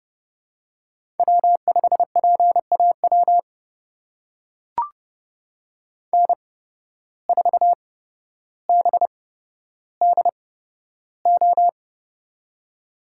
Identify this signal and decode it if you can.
Morse code, heard as a keyed tone, about 29 wpm, spaced as for 13 wpm, 700 Hz. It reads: W5PAW N4BDO
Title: USA Call Signs - 5 Characters - 8x Character Spacing - 30wpm